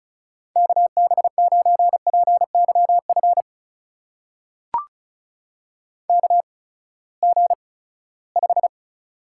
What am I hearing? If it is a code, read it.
Morse code, heard as a keyed tone, about 36 wpm, spaced as for 20 wpm, 700 Hz. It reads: KB9PYF KG5
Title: USA Call Signs - 6 Characters - 8x Character Spacing - 35wpm